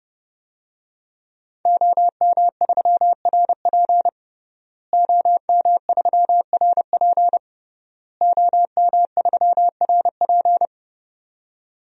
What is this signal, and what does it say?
Morse code, heard as a keyed tone, about 30 wpm, 700 Hz. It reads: OM3RP OM3RP OM3RP
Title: DXCC-100 Call Signs - 5 Characters - Repeated 3x - 30wpm